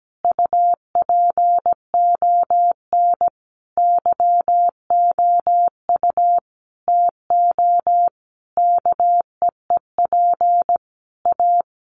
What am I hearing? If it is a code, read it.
Morse code, heard as a keyed tone, about 17 wpm, 700 Hz. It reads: UPON YOU TO KEEP A
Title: Sentences from Top 1300 Words 17wpm